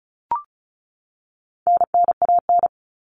Morse code, heard as a keyed tone, about 34 wpm, 700 Hz. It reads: NNAN
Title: All Cut Numbers - 4-Digit 35wpm